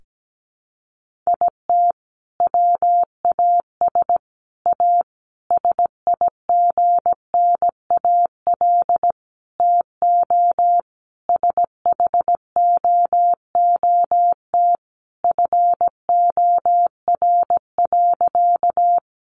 Morse code, heard as a keyed tone, about 17 wpm, 700 Hz. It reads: IT WAS A SIGNAL TO SHOOT FOR.
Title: Sentences from Top 1400 Words 17wpm